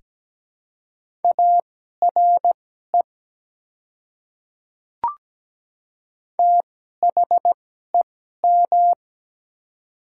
Morse code, heard as a keyed tone, about 17 wpm, spaced as for 12 wpm, 700 Hz. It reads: ARE THEM
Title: 2x Character Spacing - Top 100 Words - 17wpm